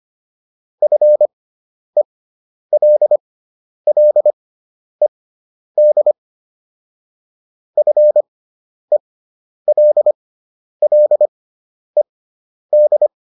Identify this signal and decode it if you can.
Morse code, heard as a keyed tone, about 25 wpm, spaced as for 10 wpm, 600 Hz. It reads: FELLED FELLED